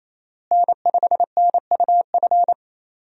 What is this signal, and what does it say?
Morse code, heard as a keyed tone, about 28 wpm, 700 Hz. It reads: N5NUF
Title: USA Call Signs - 5 Characters - Repeated 3x - 28wpm